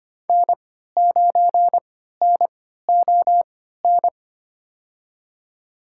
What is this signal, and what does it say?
Morse code, heard as a keyed tone, about 25 wpm, spaced as for 14 wpm, 700 Hz. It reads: N9NON
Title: USA Call Signs - 3x Character Spacing - 25wpm